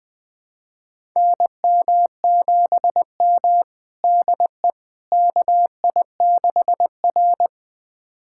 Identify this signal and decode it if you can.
Morse code, heard as a keyed tone, about 20 wpm, 700 Hz. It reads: NM7M DE KI6R
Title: USA Callsign DE Callsign - 4 Characters - 20wpm